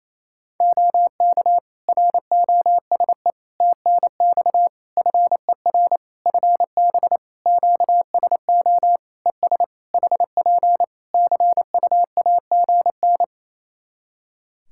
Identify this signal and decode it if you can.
Morse code, heard as a keyed tone, about 28 wpm, 700 Hz. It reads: OK ROSE TNX FER FB QSO ES HP CUAGN
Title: Practice QSO - Exchange 4 - Ending - 28wpm